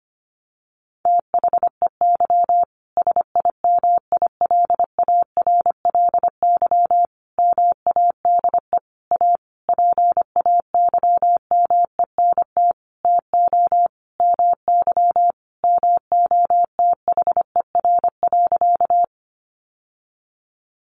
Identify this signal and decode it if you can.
Morse code, heard as a keyed tone, about 25 wpm, 700 Hz. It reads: THEY SIMILARLY MADE A PAYMENT TO MY MOTHER.
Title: Sentences from Top 1400 Words 25wpm